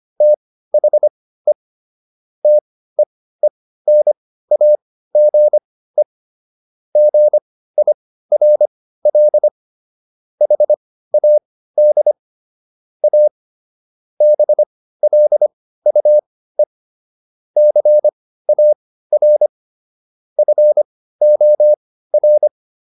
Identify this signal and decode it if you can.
Morse code, heard as a keyed tone, about 25 wpm, spaced as for 15 wpm, 600 Hz. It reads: THE TEENAGE GIRL HAD A BLUE CAR FOR